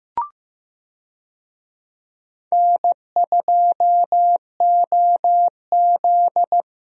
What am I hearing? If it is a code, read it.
Morse code, heard as a keyed tone, about 15 wpm, 700 Hz. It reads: N2OZ
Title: USA Call Signs - 4 Characters - Repeated 2x - 15wpm